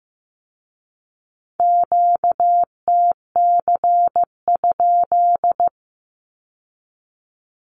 Morse code, heard as a keyed tone, about 15 wpm, 700 Hz. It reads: QTC?
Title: Q-Codes with Spoken Meaning - 15wpm